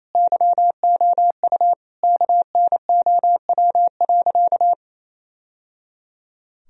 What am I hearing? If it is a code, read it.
Morse code, heard as a keyed tone, about 28 wpm, 700 Hz. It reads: YOU KNOW.
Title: Sentences from Top 1000 Words - Review Part 3 - 28wpm